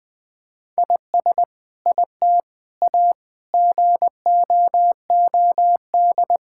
Morse code, heard as a keyed tone, about 20 wpm, 700 Hz. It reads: IS IT A GOOD
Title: Sentences from Top 1000 Words - Review Part 5 - 20wpm